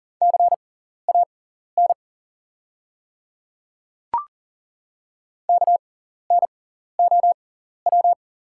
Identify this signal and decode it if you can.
Morse code, heard as a keyed tone, about 38 wpm, spaced as for 14 wpm, 700 Hz. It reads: CAN KNOW
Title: 6x Character Spacing - Top 100 Words - 40wpm